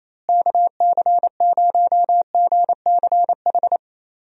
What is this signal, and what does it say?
Morse code, heard as a keyed tone, about 28 wpm, 700 Hz. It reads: KC0GCH
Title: USA Call Signs - 6 Characters - Repeated 3x - 28wpm